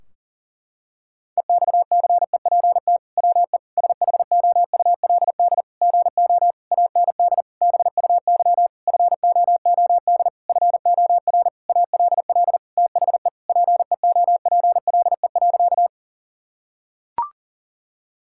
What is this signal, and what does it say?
Morse code, heard as a keyed tone, about 40 wpm, 700 Hz. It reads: EXCEPT WE SHOULD GO AND BUY FOOD FOR ALL THE PEOPLE.
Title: Sentences from Top 1600 Words 40wpm